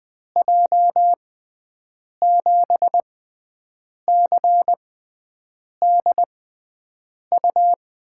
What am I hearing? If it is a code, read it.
Morse code, heard as a keyed tone, about 20 wpm, spaced as for 7 wpm, 700 Hz. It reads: J7CDU
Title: USA Call Signs - 6 Characters - 6x Character Spacing - 20wpm